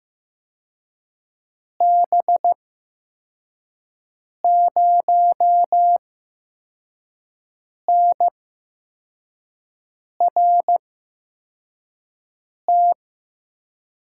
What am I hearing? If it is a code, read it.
Morse code, heard as a keyed tone, about 15 wpm, spaced as for 4 wpm, 700 Hz. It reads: B0NRT
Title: USA Call Signs - 6 Characters - 8x Character Spacing - 15wpm